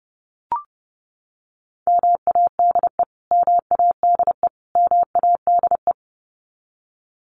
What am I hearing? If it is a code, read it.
Morse code, heard as a keyed tone, about 30 wpm, 700 Hz. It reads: MADE MADE MADE
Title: Top 500 Words Repeated 30wpm